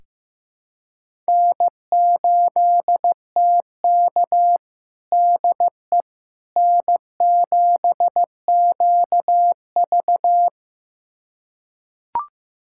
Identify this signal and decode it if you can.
Morse code, heard as a keyed tone, about 15 wpm, 700 Hz. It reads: N8TK DE N7QV
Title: USA Callsign DE Callsign - 4 Characters - 15wpm